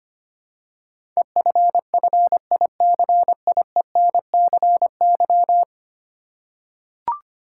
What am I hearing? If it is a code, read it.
Morse code, heard as a keyed tone, about 25 wpm, 700 Hz. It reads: EFFICIENCY
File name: Ten Letter Words - 25wpm